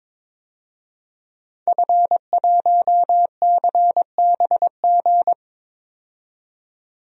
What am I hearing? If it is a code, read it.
Morse code, heard as a keyed tone, about 21 wpm, 700 Hz. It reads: F1CBG
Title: DXCC-100 Call Signs - 5 Characters - 22wpm